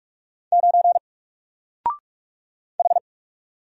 Morse code, heard as a keyed tone, about 44 wpm, 700 Hz. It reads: 9 H
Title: Single Letter-Number Alt 45wpm